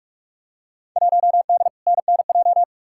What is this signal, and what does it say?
Morse code, heard as a keyed tone, about 48 wpm, 700 Hz. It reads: 1D NNJ
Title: Field Day Exchanges Encore - 45wpm